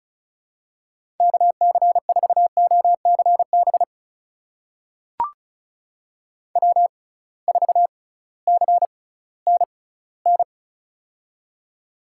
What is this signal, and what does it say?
Morse code, heard as a keyed tone, about 36 wpm, spaced as for 19 wpm, 700 Hz. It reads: KC4OCB W4CNN
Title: USA Call Signs - 6x Character Spacing - 35wpm